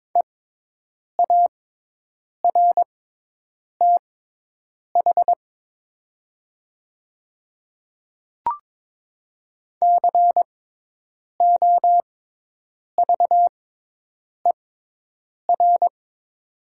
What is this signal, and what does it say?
Morse code, heard as a keyed tone, about 22 wpm, spaced as for 8 wpm, 700 Hz. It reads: EARTH COVER
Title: 6x Character Spacing - Top 300 Words - 22wpm